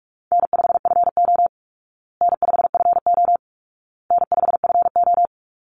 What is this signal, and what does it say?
Morse code, heard as a keyed tone, about 44 wpm, 700 Hz. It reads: N5FO N5FO N5FO
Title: Call Signs Repeated 2x - 45wpm